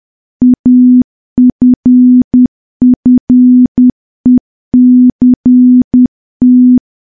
Morse code, heard as a keyed tone, about 10 wpm, 250 Hz. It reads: AFFECT
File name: morse / morse-training-10WPM